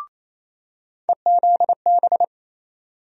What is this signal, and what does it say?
Morse code, heard as a keyed tone, about 28 wpm, 700 Hz. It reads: EZB
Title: USA Call Sign Postfixes - 3 Letters 28wpm